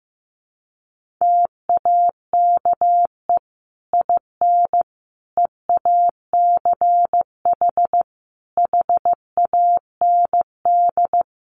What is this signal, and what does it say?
Morse code, heard as a keyed tone, about 15 wpm, 700 Hz. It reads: TAKE IN EACH HAND
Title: Sentences from Top 1400 Words 15wpm